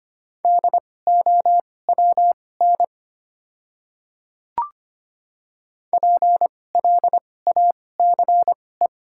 Morse code, heard as a keyed tone, about 25 wpm, spaced as for 18 wpm, 700 Hz. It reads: DOWN PLACE
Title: 2x Character Spacing - Top 200 Words - 25wpm